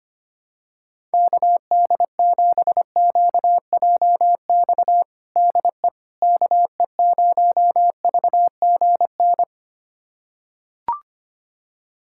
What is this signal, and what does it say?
Morse code, heard as a keyed tone, about 25 wpm, 700 Hz. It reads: KD7QJX DE KE0VGN
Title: USA Callsign DE Callsign 25wpm